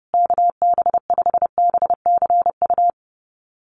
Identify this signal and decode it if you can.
Morse code, heard as a keyed tone, about 30 wpm, 700 Hz. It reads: KB5BCU
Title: USA Call Signs Repeated 2x 30wpm